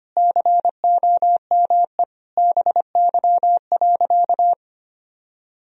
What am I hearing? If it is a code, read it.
Morse code, heard as a keyed tone, about 25 wpm, 700 Hz. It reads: COME BY.
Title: Sentences from Top 1000 Words - Review Part 3 - 25wpm